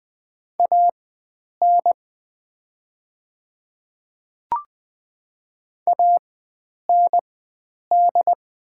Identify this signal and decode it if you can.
Morse code, heard as a keyed tone, about 20 wpm, spaced as for 9 wpm, 700 Hz. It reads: AN AND
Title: 4x Character Spacing - Top 100 Words - 20wpm